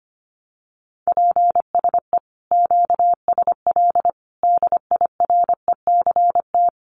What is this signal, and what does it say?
Morse code, heard as a keyed tone, about 25 wpm, 700 Hz. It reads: PSE QSL DIRECT
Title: Common QSO Phrases 25wpm